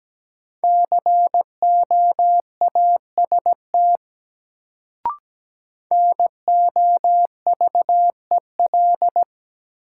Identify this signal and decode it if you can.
Morse code, heard as a keyed tone, about 17 wpm, 700 Hz. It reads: COAST NOVEL
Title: Sentences from Top 1300 Words - New Word Drill Alt - 17wpm